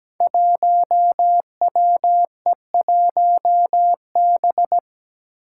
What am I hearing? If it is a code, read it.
Morse code, heard as a keyed tone, about 17 wpm, 700 Hz. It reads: 1WE1B
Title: Five Challenging Letter-Number Combinations 17wpm